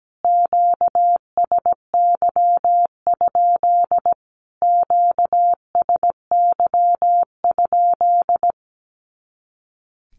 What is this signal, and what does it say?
Morse code, heard as a keyed tone, about 17 wpm, 700 Hz. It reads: QSY? QSY?
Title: Q-Codes Repeated 3x 17wpm